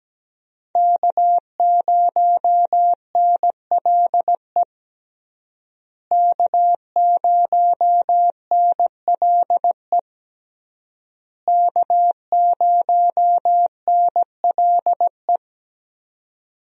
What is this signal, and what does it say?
Morse code, heard as a keyed tone, about 17 wpm, 700 Hz. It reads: K0NLE K0NLE K0NLE
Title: USA Call Signs - 5 Characters - Repeated 3x - 17wpm